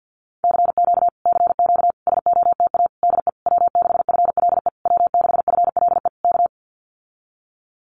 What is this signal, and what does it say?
Morse code, heard as a keyed tone, about 50 wpm, 700 Hz. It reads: CQ CQ SOTA DE W6FLE W6FLE K
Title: Common QSO Phrases Encore 50wpm